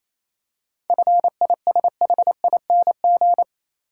Morse code, heard as a keyed tone, about 27 wpm, 690 Hz. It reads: FISHING
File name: Easy Words - Speed-Racing with Hints - 22wpm